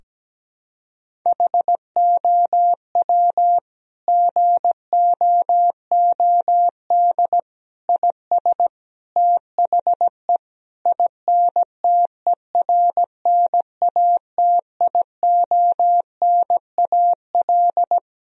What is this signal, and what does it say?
Morse code, heard as a keyed tone, about 17 wpm, 700 Hz. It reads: HOW GOOD IS THE INTERNATIONAL
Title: Sentences from Top 1000 Words - Review Part 4 - 17wpm